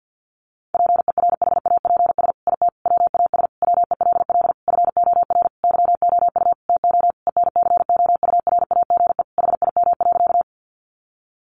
Morse code, heard as a keyed tone, about 50 wpm, 700 Hz. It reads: PERHAPS IT WAS WELL FOR YOU TO ENCOURAGE HIM.
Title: Sentences from Top 1400 Words 50wpm